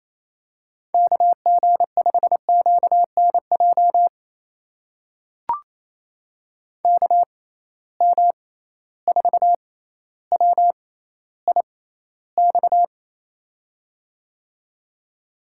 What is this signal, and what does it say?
Morse code, heard as a keyed tone, about 27 wpm, spaced as for 14 wpm, 700 Hz. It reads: KG5QNJ KM4WIX
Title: USA Call Signs - 6 Characters - 6x Character Spacing - 28wpm